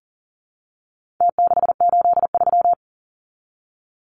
Morse code, heard as a keyed tone, about 40 wpm, 700 Hz. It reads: T683
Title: Numbers - Contest Style - 4-Digit 40wpm